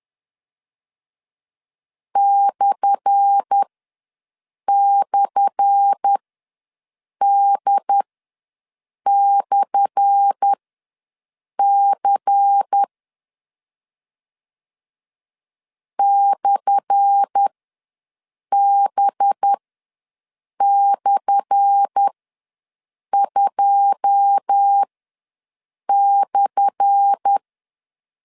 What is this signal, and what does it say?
Morse code, heard as a keyed tone, about 10 wpm, spaced as for 6 wpm, 800 Hz. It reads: //D/C /B/2/
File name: LEZIONI / 10 SIGNS / 10A SLASH ws 10wpm 4121